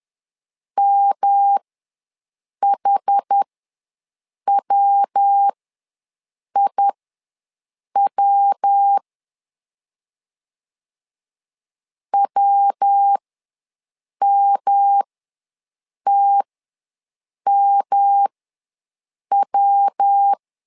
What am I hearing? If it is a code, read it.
Morse code, heard as a keyed tone, about 10 wpm, spaced as for 6 wpm, 800 Hz. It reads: MHWIW WMTMW
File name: LEZIONI / 3 MWRO / 3 MW ws 10wpm 2213